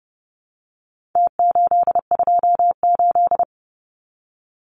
Morse code, heard as a keyed tone, about 30 wpm, 700 Hz. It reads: T828
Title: Numbers - Contest Style - 4-Digit 30wpm